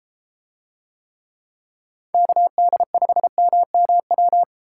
Morse code, heard as a keyed tone, about 33 wpm, 700 Hz. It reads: KD5MMW